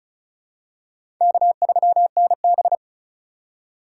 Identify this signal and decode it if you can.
Morse code, heard as a keyed tone, about 34 wpm, 700 Hz. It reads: K3NB
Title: USA Call Signs - 4 Characters - 4x Character Spacing - 35wpm